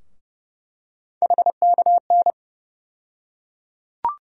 Morse code, heard as a keyed tone, about 30 wpm, 700 Hz. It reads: HKN
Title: USA Call Sign Postfixes - 3 Letters 30wpm